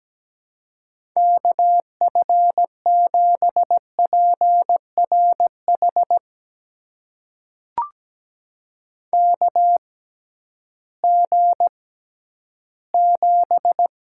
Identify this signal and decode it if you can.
Morse code, heard as a keyed tone, about 17 wpm, spaced as for 11 wpm, 700 Hz. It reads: KF7PRH KG7
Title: USA Call Signs - 6 Characters - 6x Character Spacing - 17wpm